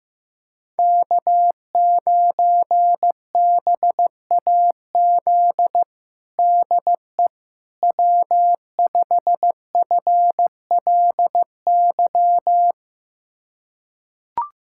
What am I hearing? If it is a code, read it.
Morse code, heard as a keyed tone, about 15 wpm, 700 Hz. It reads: K9BAZ DE W5FLY
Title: USA Callsign DE Callsign - 5 Characters - 15wpm